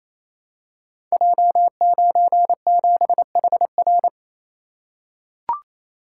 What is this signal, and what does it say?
Morse code, heard as a keyed tone, about 29 wpm, 700 Hz. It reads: J97HR
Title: Code Groups - 5 Letter:Numbers - 28wpm